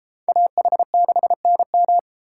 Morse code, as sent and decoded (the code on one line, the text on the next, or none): .- .... -.... -. --
AH6NM